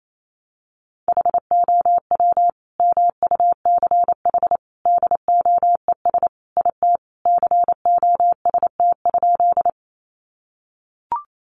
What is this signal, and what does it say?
Morse code, heard as a keyed tone, about 28 wpm, 700 Hz. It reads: HOW MUCH DOES IT COST?
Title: Sentences from Top 800 Words 28wpm